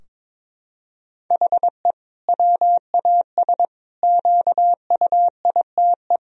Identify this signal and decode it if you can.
Morse code, heard as a keyed tone, about 22 wpm, 700 Hz. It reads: HE WAS QUITE
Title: Sentences from Top 1000 Words - Review Part 3 - 22wpm